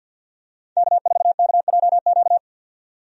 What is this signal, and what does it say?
Morse code, heard as a keyed tone, about 52 wpm, 700 Hz. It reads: K4KJQ
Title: USA Call Signs - 5 Characters - 50wpm